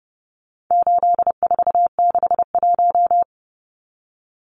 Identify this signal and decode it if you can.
Morse code, heard as a keyed tone, about 28 wpm, 700 Hz. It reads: 8461
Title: Numbers - 4-Digit 30wpm